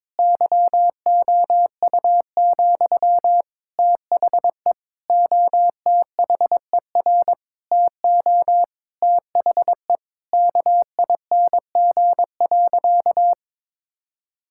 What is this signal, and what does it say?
Morse code, heard as a keyed tone, about 22 wpm, 700 Hz. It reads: YOU, THE OTHER TO THE KING.
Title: Sentences from Top 1000 Words - Review Part 6 - 22wpm